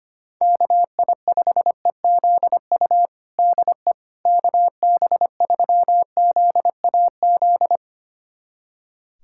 Morse code, heard as a keyed tone, about 25 wpm, 700 Hz. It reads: KI5EZU DE KB3ZAZ
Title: USA Callsign DE Callsign - 6 Characters - 25wpm